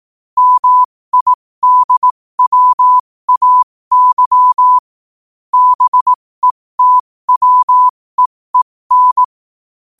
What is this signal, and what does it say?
Morse code, heard as a keyed tone, about 18 wpm, spaced as for 15 wpm, 1000 Hz. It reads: MIDWAY BETWEEN
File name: qotdfiles / 15wpm